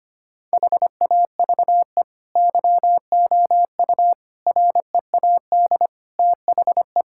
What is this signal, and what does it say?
Morse code, heard as a keyed tone, about 25 wpm, 700 Hz. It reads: HAVE YOU READ THE
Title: Sentences from Top 1000 Words - Review Part 5 - 25wpm